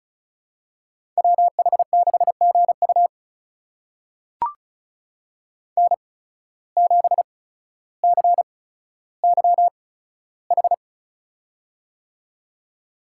Morse code, heard as a keyed tone, about 36 wpm, spaced as for 15 wpm, 700 Hz. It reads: WH6GU N7CYH
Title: USA Call Signs - 5 Characters - 8x Character Spacing - 35wpm